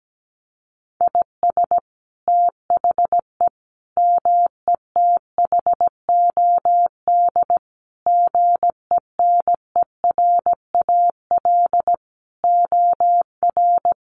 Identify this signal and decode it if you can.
Morse code, heard as a keyed tone, about 17 wpm, 700 Hz. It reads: IS THE METHOD GENERAL OR
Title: Sentences from Top 1500 Words 17wpm